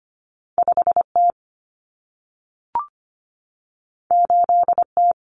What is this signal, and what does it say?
Morse code, heard as a keyed tone, about 25 wpm, 700 Hz. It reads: 5T 8T
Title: Common Cut Numbers - 2-Digit 25wpm